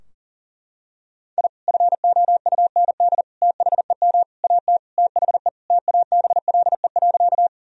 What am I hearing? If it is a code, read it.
Morse code, heard as a keyed tone, about 40 wpm, 700 Hz. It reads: I FOUND THEM AT THE TABLE.
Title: Sentences from Top 1000 Words - Review Part 2 - 40wpm